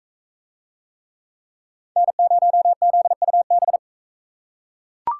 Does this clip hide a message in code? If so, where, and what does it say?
Yes: Morse code from 2.0 to 3.8 s, heard as a keyed tone, about 42 wpm, 700 Hz. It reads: N0ZUB